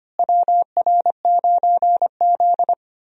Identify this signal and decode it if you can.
Morse code, heard as a keyed tone, about 25 wpm, 700 Hz. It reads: WR9Z
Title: USA Call Signs - 4 Characters - Repeated 2x - 25wpm